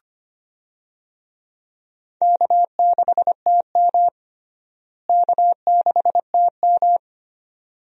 Morse code, heard as a keyed tone, about 25 wpm, 700 Hz. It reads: K6TM K6TM
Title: USA Call Signs - 4 Characters - Repeated 2x - 25wpm